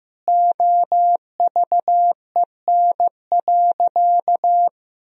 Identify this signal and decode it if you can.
Morse code, heard as a keyed tone, about 15 wpm, 700 Hz. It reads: OVEN.
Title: Sentences from Top 1000 Words - Review Part 4 - 15wpm